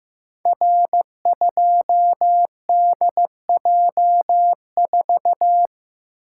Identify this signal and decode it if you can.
Morse code, heard as a keyed tone, about 15 wpm, 700 Hz. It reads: R2DJ4
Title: Code Groups - 5 Letter:Numbers - 15wpm